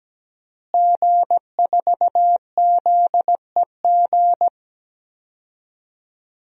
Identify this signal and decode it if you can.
Morse code, heard as a keyed tone, about 17 wpm, 700 Hz. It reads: G4ZEG
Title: DXCC-100 Call Signs - 5 Characters - Repeated 3x - 17wpm